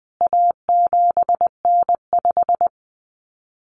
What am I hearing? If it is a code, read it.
Morse code, heard as a keyed tone, about 20 wpm, 700 Hz. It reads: A7N5
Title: CWT Exchange 20wpm